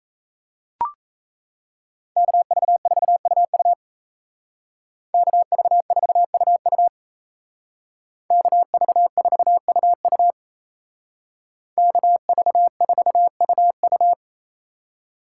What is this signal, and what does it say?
Morse code, heard as a keyed tone, about 33 wpm, 700 Hz. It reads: KV4UU KV4UU KV4UU KV4UU
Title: Speed Racing - USA 2x2 Call Signs 28wpm